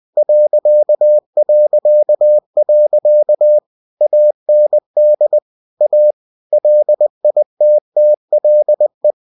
Morse code, heard as a keyed tone, about 20 wpm, 600 Hz. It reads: ... AND A LITTLE
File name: cw-books / winnie-the-pooh / winnie-20wpm-page-0003